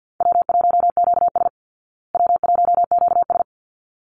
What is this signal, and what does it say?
Morse code, heard as a keyed tone, about 48 wpm, 700 Hz. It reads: W1QS W1QS
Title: Call Signs Repeated 3x - 50wpm